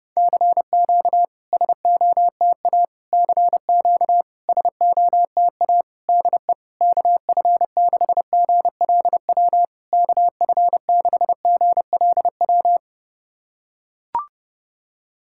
Morse code, heard as a keyed tone, about 30 wpm, 700 Hz. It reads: CQ SOTA CQ SOTA DE KF6GLW KF6GLW
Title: SOTA QSO Practice 30wpm